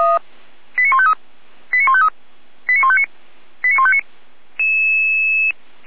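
Five-tone selective call (ZVEI), heard as quick runs of five tones, tones of about 70 ms. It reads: 89153 89153 89159 89159
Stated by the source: ZVEI five-tone selective call received on an SDR.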